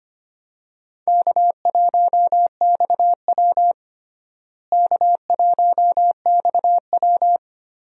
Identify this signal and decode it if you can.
Morse code, heard as a keyed tone, about 25 wpm, 700 Hz. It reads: K1XW K1XW
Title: USA Call Signs - 4 Characters - Repeated 2x - 25wpm